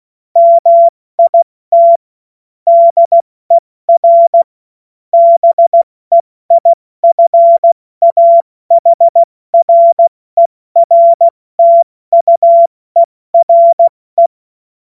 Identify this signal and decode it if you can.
Morse code, heard as a keyed tone, about 16 wpm, spaced as for 14 wpm, 680 Hz. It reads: MIT DER BEIFAHRERTUERE
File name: Morsekurs / Witze / Witz 20. September 2022 70 F80